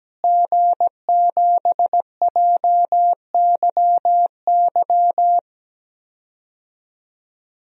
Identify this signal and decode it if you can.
Morse code, heard as a keyed tone, about 17 wpm, 700 Hz. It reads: G7JYY
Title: DXCC-100 Call Signs - 5 Characters - Repeated 2x - 17wpm